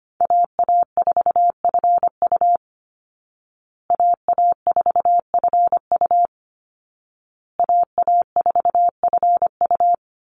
Morse code, heard as a keyed tone, about 25 wpm, 700 Hz. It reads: AA4FU AA4FU AA4FU
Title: Call Signs Repeated 2x - 25wpm